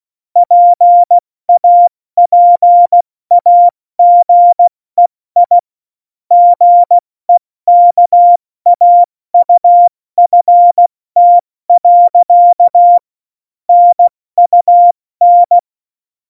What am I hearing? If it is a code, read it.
Morse code, heard as a keyed tone, about 16 wpm, 700 Hz. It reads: PAPAGEI GEKAUFT. NUN
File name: Morsekurs / Witze / Witz 15. August 17, Tempo 70 F80